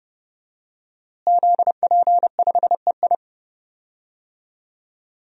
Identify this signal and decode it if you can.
Morse code, heard as a keyed tone, about 30 wpm, 700 Hz. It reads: ZP5EI
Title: DXCC-100 Call Signs - 5 Characters - Repeated 2x - 30wpm